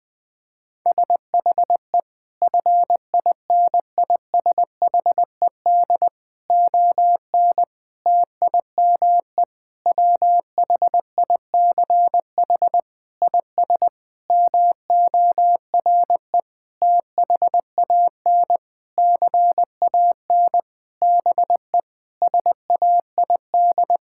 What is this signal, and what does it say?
Morse code, heard as a keyed tone, about 21 wpm, 700 Hz. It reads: SHE FINISHED ON TIME WHICH IS MORE THAN CAN BE SAID
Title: Sentences with 8-Word Phrases - 20wpm